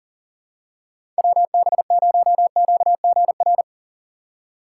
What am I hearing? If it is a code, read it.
Morse code, heard as a keyed tone, about 40 wpm, 700 Hz. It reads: WB0QGR
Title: USA Call Signs - 6 Characters - 3x Character Spacing - 40wpm